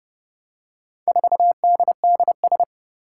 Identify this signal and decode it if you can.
Morse code, heard as a keyed tone, about 30 wpm, 700 Hz. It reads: 4DDS
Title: Four Challenging Letter-Number Combinations 30wpm